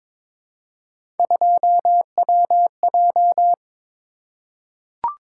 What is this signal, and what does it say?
Morse code, heard as a keyed tone, about 22 wpm, 700 Hz. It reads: 2WJ
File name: practice / 22wpm / Three Letter-Number Combinations - 22wpm